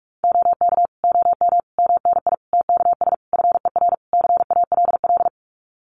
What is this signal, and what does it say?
Morse code, heard as a keyed tone, about 44 wpm, 700 Hz. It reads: OK OM MNI TKS FER CALL
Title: Common QSO Phrases 45wpm